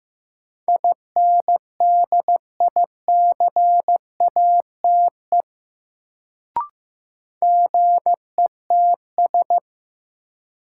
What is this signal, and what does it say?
Morse code, heard as a keyed tone, about 16 wpm, 700 Hz. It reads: INDICATE GETS
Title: Sentences from Top 1300 Words - New Word Drill Alt - 15wpm